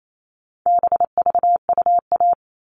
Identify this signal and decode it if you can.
Morse code, heard as a keyed tone, about 28 wpm, 700 Hz. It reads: BVUA
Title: All Cut Numbers - 4-Digit 28wpm